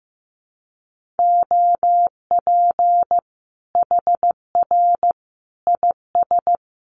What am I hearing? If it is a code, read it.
Morse code, heard as a keyed tone, about 15 wpm, 700 Hz. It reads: OP HR IS
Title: Common QSO Phrases 15wpm